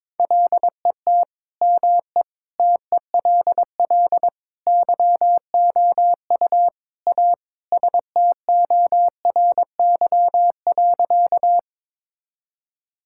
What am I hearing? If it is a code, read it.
Morse code, heard as a keyed tone, about 22 wpm, 700 Hz. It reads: LET ME TELL YOU A STORY.
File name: practice / 22wpm / Sentences with 4-Word Phrases - 22wpm